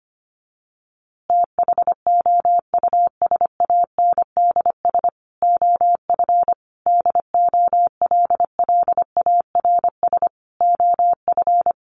Morse code, heard as a keyed tone, about 25 wpm, 700 Hz. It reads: THOUSANDS OF DOLLARS OF
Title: A Sound Of Thunder 25wpm